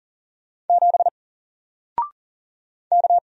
Morse code, heard as a keyed tone, about 40 wpm, 700 Hz. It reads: 7 K
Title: Single Character (Letter:Number:Punctuation:Prosign) Alt 40wpm